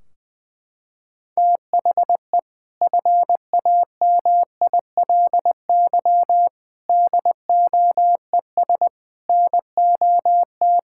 Morse code, heard as a keyed tone, about 20 wpm, 700 Hz. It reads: THE FAMILY DOES NOT
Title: Sentences from Top 1600 Words 20wpm